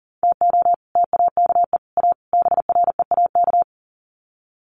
Morse code, heard as a keyed tone, about 40 wpm, 700 Hz. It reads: TO TAKE A BREAK
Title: Sets of 4 Words 40wpm